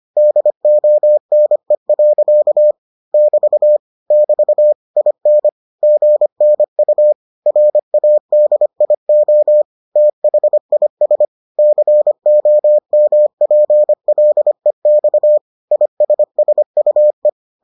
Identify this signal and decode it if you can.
Morse code, heard as a keyed tone, about 25 wpm, 600 Hz. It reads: DONE. <BT> <BT> IN GNU RADIO THIS COMPLEX ISSUE